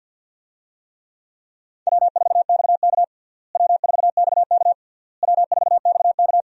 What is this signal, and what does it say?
Morse code, heard as a keyed tone, about 53 wpm, 700 Hz. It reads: W4XK W4XK W4XK
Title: USA Call Signs - 4 Characters - Repeated 3x - 50wpm